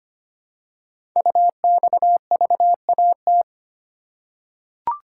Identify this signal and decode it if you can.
Morse code, heard as a keyed tone, about 25 wpm, 700 Hz. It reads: UXVAT
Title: Code Groups - 5 Letters - 25wpm